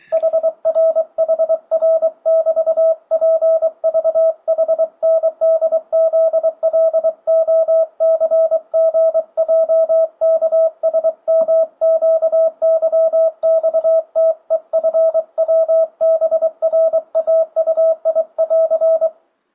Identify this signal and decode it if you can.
Morse code, heard as a keyed tone, about 23 wpm, 650 Hz. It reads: HRHR<BT>PVHNDZLOCGJKSMQYXTEFWBRAUI<AR>
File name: 2025cw / speedup / 24wpm